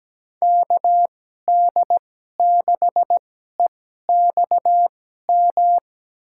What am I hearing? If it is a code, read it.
Morse code, heard as a keyed tone, about 17 wpm, spaced as for 12 wpm, 700 Hz. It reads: KD6EXM
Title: USA Call Signs - 6 Characters - 2x Character Spacing - 17wpm